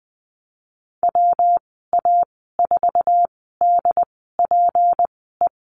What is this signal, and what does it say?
Morse code, heard as a keyed tone, about 20 wpm, spaced as for 15 wpm, 700 Hz. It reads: WA4DPE
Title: 173 Lesson 28g - Number 9 - Call Signs 20wpm